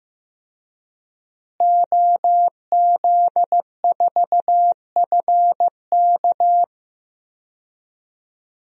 Morse code, heard as a keyed tone, about 16 wpm, 700 Hz. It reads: OZ4FK
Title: DXCC-100 Call Signs - 5 Characters - Repeated 2x - 15wpm